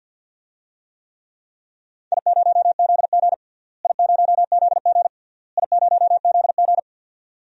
Morse code, heard as a keyed tone, about 52 wpm, 700 Hz. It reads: I0ZG I0ZG I0ZG
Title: DXCC-100 Call Signs - 4 Characters - Repeated 3x - 50wpm